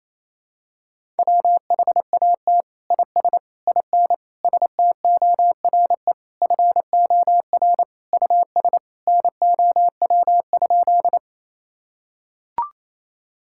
Morse code, heard as a keyed tone, about 28 wpm, 700 Hz. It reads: WHAT IS IN STORE FOR US NOW?
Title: Sentences from Top 1000 Words - Review Part 5 - 28wpm